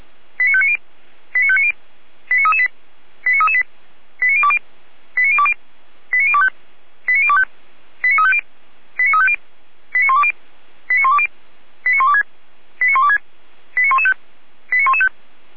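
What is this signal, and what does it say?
Five-tone selective call (ZVEI), heard as quick runs of five tones, tones of about 80 ms. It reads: 89590 89590 89338 89338 89020 89020 89025 89025 89350 89350 89120 89120 89126 89126 89115 89115
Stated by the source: ZVEI five-tone selective call received by SDR.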